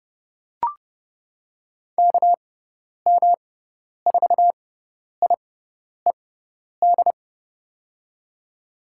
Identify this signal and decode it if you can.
Morse code, heard as a keyed tone, about 30 wpm, spaced as for 10 wpm, 700 Hz. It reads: KM4IED
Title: USA Call Signs - 6 Characters - 6x Character Spacing - 30wpm